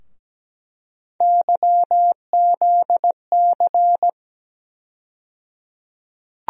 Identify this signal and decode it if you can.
Morse code, heard as a keyed tone, about 17 wpm, 700 Hz. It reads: YZC
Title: USA Call Sign Postfixes - 3 Letters 17wpm